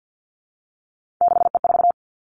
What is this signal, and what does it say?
Morse code, heard as a keyed tone, about 50 wpm, 700 Hz. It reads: 6E4
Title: Three Challenging Letter-Number Combinations 50wpm